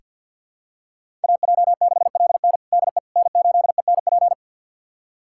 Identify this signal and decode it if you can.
Morse code, heard as a keyed tone, about 47 wpm, 700 Hz. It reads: AJ6LN DE N8ENP
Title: USA Callsign DE Callsign - 5 Characters - 50wpm